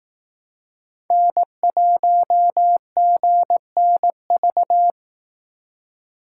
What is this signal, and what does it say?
Morse code, heard as a keyed tone, about 18 wpm, 700 Hz. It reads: N1GNV
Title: Speed Racing - USA 1x3 Call Signs 15wpm